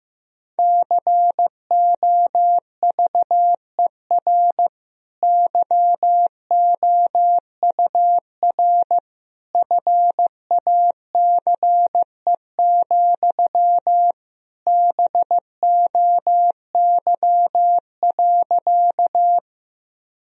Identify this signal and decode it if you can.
Morse code, heard as a keyed tone, about 16 wpm, 700 Hz. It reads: COVER YOUR FACE, BOY.
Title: Sentences from Top 1000 Words - Review Part 1 - 15wpm